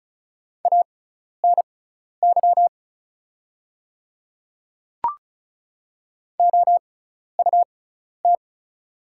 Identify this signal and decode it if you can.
Morse code, heard as a keyed tone, about 36 wpm, spaced as for 12 wpm, 690 Hz. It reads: ANY OUT
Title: 6x Character Spacing - Top 200 Words - 35wpm